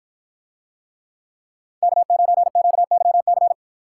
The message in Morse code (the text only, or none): K9QYC